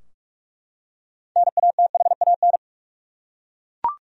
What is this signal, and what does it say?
Morse code, heard as a keyed tone, about 41 wpm, 700 Hz. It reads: NATHAN
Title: USA Common Names - 45wpm